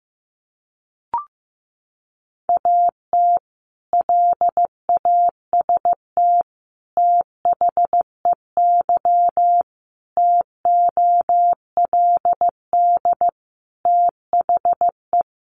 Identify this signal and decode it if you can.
Morse code, heard as a keyed tone, about 15 wpm, 700 Hz. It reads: AT LAST THEY TOLD THE
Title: Sentences from Top 1200 Words 15wpm